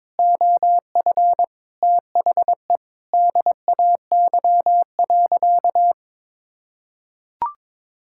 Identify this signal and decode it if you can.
Morse code, heard as a keyed tone, about 22 wpm, 700 Hz. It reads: OF THE DAY.
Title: Sentences from Top 1600 Words 22wpm